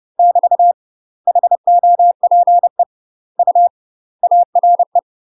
Morse code, heard as a keyed tone, about 30 wpm, 700 Hz. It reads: <BT> HOPE U ARE
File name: UK MASTER QSOs TRAINING / 30WPM / UK MASTER QSOs 0000